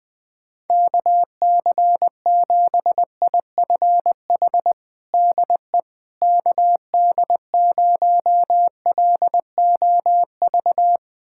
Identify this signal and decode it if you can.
Morse code, heard as a keyed tone, about 20 wpm, 700 Hz. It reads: KC7IFH DE KD0LOV
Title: USA Callsign DE Callsign 20wpm